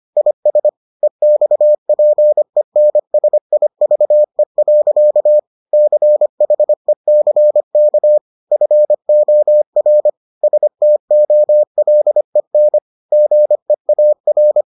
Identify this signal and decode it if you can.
Morse code, heard as a keyed tone, about 25 wpm, 600 Hz. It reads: IS EXPENSIVE. CHECK FOR STOLEN GEAR